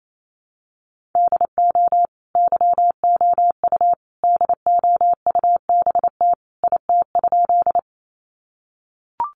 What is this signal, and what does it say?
Morse code, heard as a keyed tone, about 28 wpm, 700 Hz. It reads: DO YOU DOUBT IT?
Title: Sentences from Top 1500 Words 28wpm